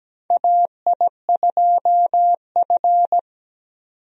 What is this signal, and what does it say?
Morse code, heard as a keyed tone, about 17 wpm, 700 Hz. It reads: AI2F
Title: USA Call Signs - 4 Characters - 17wpm